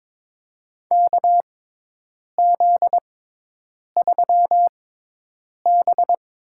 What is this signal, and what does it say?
Morse code, heard as a keyed tone, about 22 wpm, spaced as for 8 wpm, 700 Hz. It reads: KZ3B
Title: USA Call Signs - 4 Characters - 6x Character Spacing - 22wpm